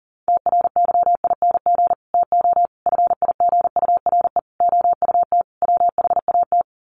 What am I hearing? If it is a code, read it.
Morse code, heard as a keyed tone, about 40 wpm, 700 Hz. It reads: TRYING TO FIGURE OUT WHAT
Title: Sets of 5 Words 40wpm